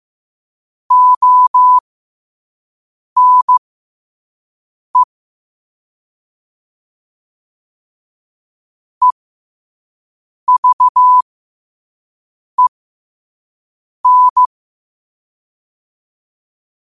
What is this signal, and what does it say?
Morse code, heard as a keyed tone, about 15 wpm, spaced as for 5 wpm, 1000 Hz. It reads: ONE EVEN
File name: qotdfiles / 5wpm